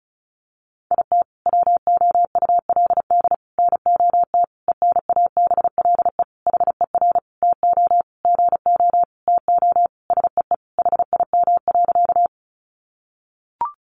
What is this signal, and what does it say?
Morse code, heard as a keyed tone, about 34 wpm, 700 Hz. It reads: IT WOULD NOT ENABLE HER TO GO TO SEE HIM.
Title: Sentences from Top 1500 Words 35wpm